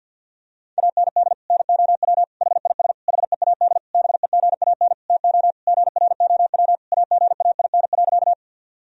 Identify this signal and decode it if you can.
Morse code, heard as a keyed tone, about 49 wpm, 700 Hz. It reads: AND NOW HIS HEAD BEGAN TO GROW AGAIN.
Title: Sentences from Top 1000 Words - Review Part 1 - 50wpm